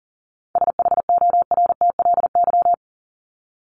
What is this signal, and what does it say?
Morse code, heard as a keyed tone, about 40 wpm, 700 Hz. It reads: SHORTLY
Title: LY Suffix - Top 100 Words with Base Words 40wpm